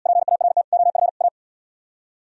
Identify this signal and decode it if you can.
Morse code, heard as a keyed tone, about 77 wpm, 700 Hz. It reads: PART OF A